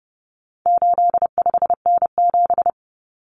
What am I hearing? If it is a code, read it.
Morse code, heard as a keyed tone, about 30 wpm, 700 Hz. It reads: 85N7
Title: Common Cut Numbers - 4-Digit 30wpm